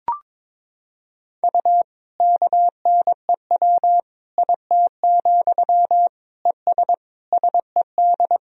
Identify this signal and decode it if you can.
Morse code, heard as a keyed tone, about 22 wpm, 700 Hz. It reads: U KNEW IT, ES SED
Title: Sentences from Top 500 Words and Abbreviations 22wpm